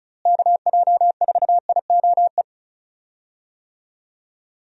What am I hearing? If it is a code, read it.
Morse code, heard as a keyed tone, about 34 wpm, 700 Hz. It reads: KJ4IOE
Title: USA Call Signs - 6 Characters - Repeated 2x - 35wpm